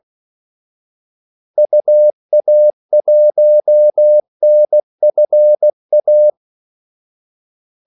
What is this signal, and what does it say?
Morse code, heard as a keyed tone, about 16 wpm, 600 Hz. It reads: UA1NFA
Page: Kaldesignaler 16 wpm